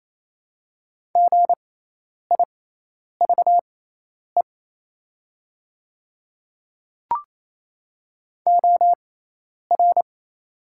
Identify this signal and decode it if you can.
Morse code, heard as a keyed tone, about 27 wpm, spaced as for 10 wpm, 700 Hz. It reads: GIVE OR
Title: 6x Character Spacing - Top 200 Words - 28wpm